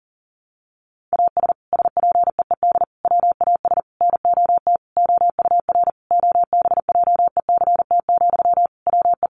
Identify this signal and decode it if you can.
Morse code, heard as a keyed tone, about 40 wpm, 700 Hz. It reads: AS SPEED WAS NOT OUR OBJECT, WE
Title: Sentences from Top 800 Words 40wpm